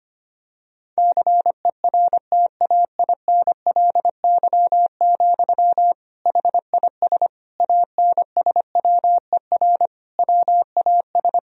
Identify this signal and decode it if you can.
Morse code, heard as a keyed tone, about 25 wpm, 700 Hz. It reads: CERTAINLY, HIS ANSWER WAS
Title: Sentences from Top 1000 Words - Review Part 2 - 25wpm